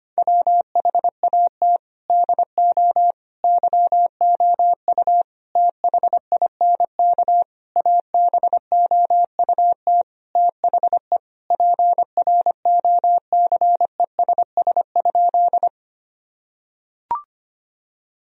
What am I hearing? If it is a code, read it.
Morse code, heard as a keyed tone, about 25 wpm, 700 Hz. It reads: WHAT DO YOU THINK ABOUT THE PROCESS?
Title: Sentences from Top 1000 Words - Review Part 3 - 25wpm